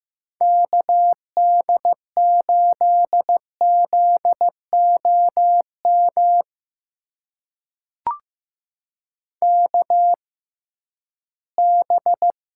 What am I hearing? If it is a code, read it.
Morse code, heard as a keyed tone, about 15 wpm, spaced as for 11 wpm, 700 Hz. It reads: KD8ZOM KB
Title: USA Call Signs - 6 Characters - 6x Character Spacing - 15wpm